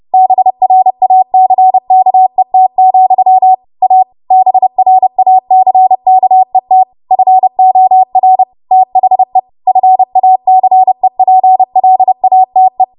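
Morse code, heard as a keyed tone, about 30 wpm, 750 Hz. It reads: BRACKET, A BRACKET FOR THE FACEPLATE
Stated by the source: Morse practice transmission at 30 WPM.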